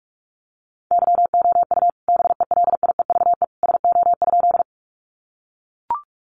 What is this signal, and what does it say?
Morse code, heard as a keyed tone, about 44 wpm, 700 Hz. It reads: YOU BELIEVE SO?
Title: Sentences from Top 1000 Words 45wpm